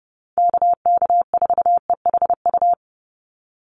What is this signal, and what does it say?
Morse code, heard as a keyed tone, about 30 wpm, 700 Hz. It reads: KK4EHU
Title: USA Call Signs 30wpm